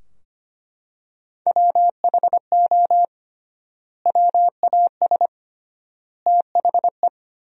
Morse code, heard as a keyed tone, about 25 wpm, 700 Hz. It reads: WHO WAS THE
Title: 3x Word-Spacing - Sentences from Top 300 Words - 25wpm